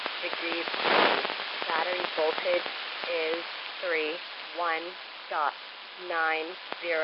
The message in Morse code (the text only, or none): none